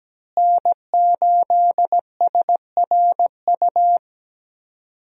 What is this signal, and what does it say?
Morse code, heard as a keyed tone, about 17 wpm, 700 Hz. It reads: N8SRU
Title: USA Call Signs - 5 Characters - Repeated 2x - 17wpm